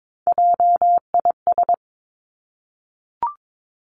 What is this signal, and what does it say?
Morse code, heard as a keyed tone, about 22 wpm, 700 Hz. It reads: JIS
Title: Three Challenging Letter-Number Combinations 22wpm